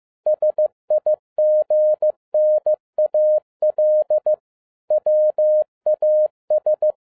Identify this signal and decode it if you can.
Morse code, heard as a keyed tone, about 15 wpm, 610 Hz. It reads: SIGNAL WAS